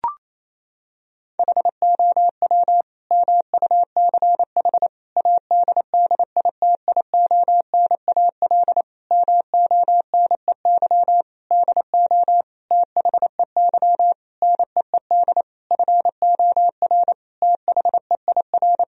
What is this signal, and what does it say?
Morse code, heard as a keyed tone, about 28 wpm, 700 Hz. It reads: HOW MUCH ADDITIONAL MONEY DO THEY NEED FOR THEIR
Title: Sentences from Top 1000 Words - Review Part 4 - 28wpm